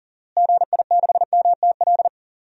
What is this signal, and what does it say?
Morse code, heard as a keyed tone, about 40 wpm, 700 Hz. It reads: GI6MTL
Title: DXCC-100 Call Signs - 6 Characters - Repeated 3x - 40wpm